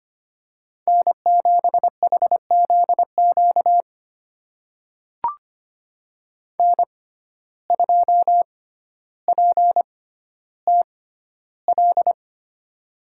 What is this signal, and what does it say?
Morse code, heard as a keyed tone, about 25 wpm, spaced as for 13 wpm, 700 Hz. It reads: N7HZQ N2PTL
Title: USA Call Signs - 5 Characters - 6x Character Spacing - 25wpm